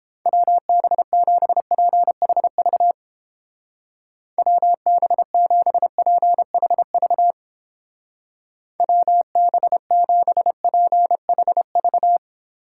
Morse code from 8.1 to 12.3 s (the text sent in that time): WB7PHV